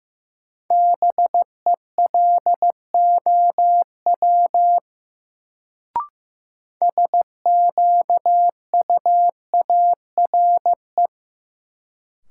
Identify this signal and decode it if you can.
Morse code, heard as a keyed tone, about 15 wpm, 700 Hz. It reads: BELOW SQUARE
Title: Sentences from Top 1200 Words - New Word Drill Alt - 15wpm